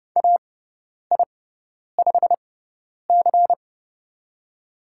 Morse code, heard as a keyed tone, about 30 wpm, spaced as for 10 wpm, 700 Hz. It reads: AI5C
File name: Lesson 16g - Letter c - Call Signs 30wpm